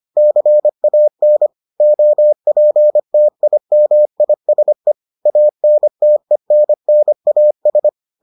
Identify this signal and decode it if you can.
Morse code, heard as a keyed tone, about 25 wpm, 600 Hz. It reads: CAN OPTIMISE ANTENNAS